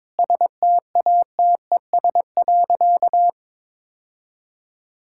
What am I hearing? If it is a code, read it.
Morse code, heard as a keyed tone, about 22 wpm, 700 Hz. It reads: STATES.
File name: practice / 22wpm / Sentences with 3-Word Phrases - 22wpm